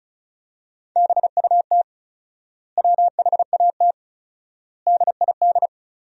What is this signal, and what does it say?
Morse code, heard as a keyed tone, about 34 wpm, 700 Hz. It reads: BUT WHAT DID
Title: 4x Word-Spacing - Sentences from Top 200 Words - 35wpm